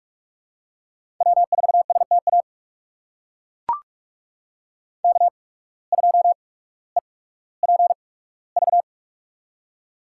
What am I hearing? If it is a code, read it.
Morse code, heard as a keyed tone, about 42 wpm, spaced as for 19 wpm, 700 Hz. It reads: W4STA K2EPV
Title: USA Call Signs - 5 Characters - 8x Character Spacing - 45wpm